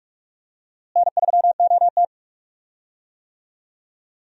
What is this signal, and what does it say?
Morse code, heard as a keyed tone, about 42 wpm, 700 Hz. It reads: N3OT
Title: USA Call Signs - 4 Characters - Repeated 3x - 45wpm